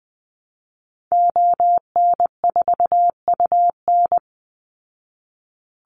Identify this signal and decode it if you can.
Morse code, heard as a keyed tone, about 19 wpm, 700 Hz. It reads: ON4UN
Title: Call Signs Repeated 3x - 20wpm